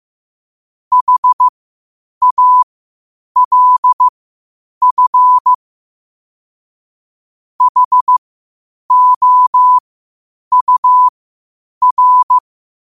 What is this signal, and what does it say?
Morse code, heard as a keyed tone, about 15 wpm, spaced as for 8 wpm, 1000 Hz. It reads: HALF HOUR
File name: qotdfiles / 8wpm